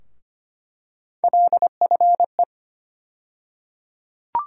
Morse code, heard as a keyed tone, about 25 wpm, 700 Hz. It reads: LFE
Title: USA Call Sign Postfixes - 3 Letters 25wpm